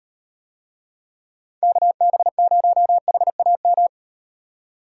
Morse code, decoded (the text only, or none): KB0HAM